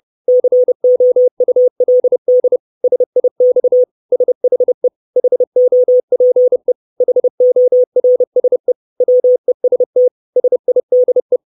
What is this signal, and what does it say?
Morse code, heard as a keyed tone, about 30 wpm, 500 Hz. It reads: COULD SIX SHE HOPE HORSE WEST SIDE
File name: 632 Words 30WPM